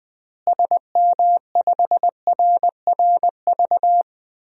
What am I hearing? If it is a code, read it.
Morse code, heard as a keyed tone, about 20 wpm, 700 Hz. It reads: SM5RRV
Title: DXCC-100 Call Signs - 6 Characters - Repeated 3x - 20wpm